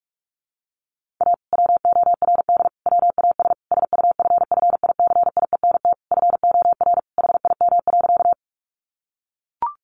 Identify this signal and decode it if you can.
Morse code, heard as a keyed tone, about 44 wpm, 700 Hz. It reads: A WORD WAS SUFFICIENT FOR HIM.
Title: Sentences from Top 1400 Words 45wpm